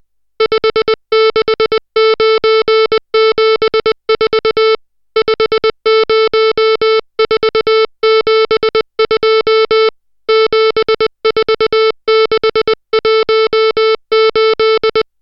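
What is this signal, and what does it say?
Morse code, heard as a keyed tone, about 20 wpm, 440 Hz, with noise. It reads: 56974 50472 74618
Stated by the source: Morse code drill of digits at 60 characters per minute.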